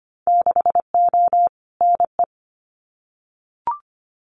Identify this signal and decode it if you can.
Morse code, heard as a keyed tone, about 25 wpm, 700 Hz. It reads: 6O NE
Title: Winter Field Day Exchanges Encore 25wpm